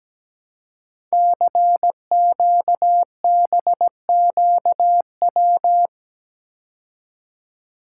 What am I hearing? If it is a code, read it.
Morse code, heard as a keyed tone, about 17 wpm, 700 Hz. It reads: CQBQW
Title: Code Groups - 5 Letter:Numbers - 17wpm